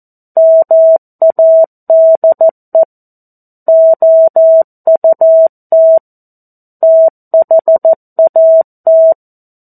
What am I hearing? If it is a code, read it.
Morse code, heard as a keyed tone, about 14 wpm, 650 Hz. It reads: MADE OUT THAT
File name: 02-Chapter-02 14 WPM